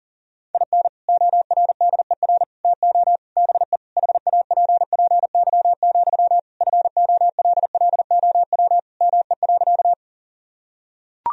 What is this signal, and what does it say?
Morse code, heard as a keyed tone, about 40 wpm, 700 Hz. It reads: IN ORDER TO BE HAPPY, FOLLOW ME.
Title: Sentences from Top 1000 Words - Review Part 2 - 40wpm